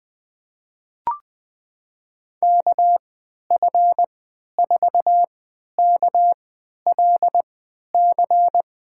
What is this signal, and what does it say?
Morse code, heard as a keyed tone, about 21 wpm, spaced as for 12 wpm, 700 Hz. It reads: KF4KLC
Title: USA Call Signs - 6 Characters - 3x Character Spacing - 20wpm